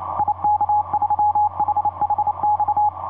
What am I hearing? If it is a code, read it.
Morse code, heard as a keyed tone, about 28 wpm, 840 Hz. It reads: IK3HHX
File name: tx3d ik3hhx etc 30m cw 31jan2010 0932